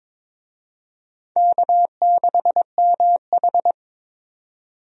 Morse code, heard as a keyed tone, about 22 wpm, 700 Hz. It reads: K6MH
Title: USA Call Signs - 4 Characters - 22wpm